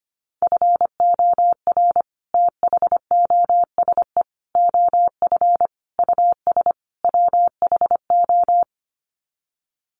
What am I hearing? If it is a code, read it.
Morse code, heard as a keyed tone, about 25 wpm, 700 Hz. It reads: FOR THOSE OF US WHO
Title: Sets of 5 Words Encore 25wpm